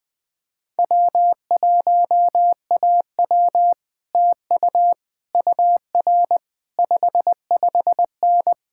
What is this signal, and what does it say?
Morse code, heard as a keyed tone, about 20 wpm, 700 Hz. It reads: W1AW TU UR 55N
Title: Speed-Racing Common QSO Phrases Encore - 17wpm